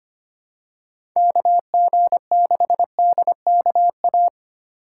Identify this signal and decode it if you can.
Morse code, heard as a keyed tone, about 25 wpm, 700 Hz. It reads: KG6DKA
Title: USA Call Signs - 6 Characters - 2x Character Spacing - 25wpm